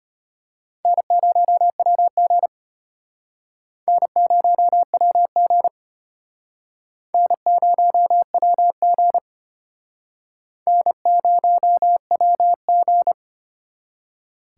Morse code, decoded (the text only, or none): N0WG N0WG N0WG N0WG